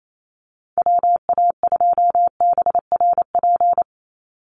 Morse code, heard as a keyed tone, about 28 wpm, 700 Hz. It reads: WA2BRP
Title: USA Call Signs 28wpm